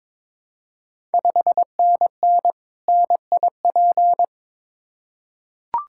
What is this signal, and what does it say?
Morse code, heard as a keyed tone, about 22 wpm, 700 Hz. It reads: 5NN NIP
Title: Ontario QSO Party Exchange 22wpm